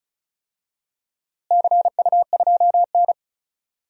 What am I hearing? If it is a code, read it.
Morse code, heard as a keyed tone, about 36 wpm, 700 Hz. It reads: CU2N
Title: DXCC-100 Call Signs - 4 Characters - Repeated 2x - 35wpm